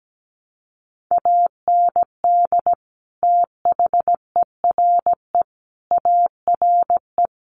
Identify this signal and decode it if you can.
Morse code, heard as a keyed tone, about 17 wpm, 700 Hz. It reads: AND THERE ARE
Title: Sentences from Top 300 Words 17wpm